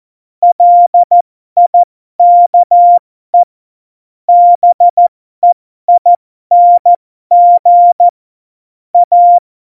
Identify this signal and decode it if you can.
Morse code, heard as a keyed tone, about 14 wpm, spaced as for 12 wpm, 700 Hz. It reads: LIKE BEING A
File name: Morsekurs / Witze